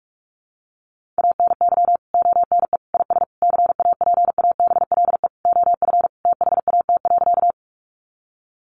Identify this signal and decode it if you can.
Morse code, heard as a keyed tone, about 44 wpm, 700 Hz. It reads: ANY ONE IS CAPABLE OF THAT.
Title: Sentences from Top 1200 Words 45wpm